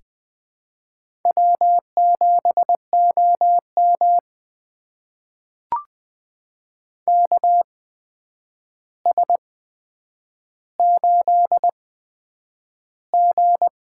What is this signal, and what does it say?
Morse code, heard as a keyed tone, about 20 wpm, spaced as for 9 wpm, 700 Hz. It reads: W7OM KS8G
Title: USA Call Signs - 4 Characters - 8x Character Spacing - 20wpm